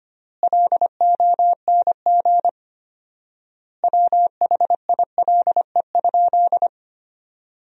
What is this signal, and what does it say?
Morse code, heard as a keyed tone, about 25 wpm, 700 Hz. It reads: LONG WHILE?
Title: 4x Word-Spacing - Sentences from Top 300 Words - 25wpm